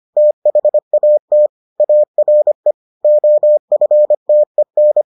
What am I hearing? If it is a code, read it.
Morse code, heard as a keyed tone, about 25 wpm, 600 Hz. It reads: THAT ARE OFTEN